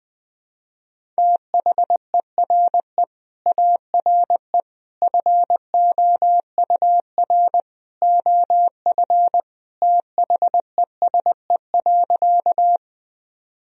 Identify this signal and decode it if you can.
Morse code, heard as a keyed tone, about 21 wpm, 700 Hz. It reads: THERE ARE FOUR OF THESE.
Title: Sentences from Top 1000 Words - Review Part 1 - 20wpm